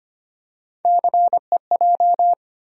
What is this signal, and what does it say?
Morse code, heard as a keyed tone, about 25 wpm, 700 Hz. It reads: CEJ
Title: USA Call Sign Postfixes - 3 Letters 25wpm